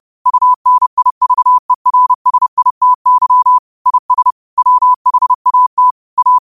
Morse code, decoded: UNIVERSITY IS WHAT A

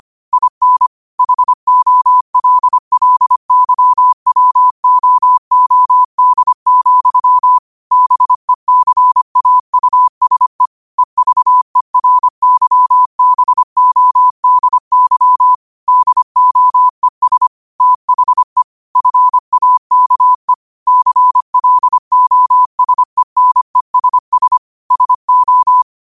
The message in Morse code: IN HOLLYWOOD, BECAUSE EVERYBODY DOES THE FAKE CLOSENESS SO

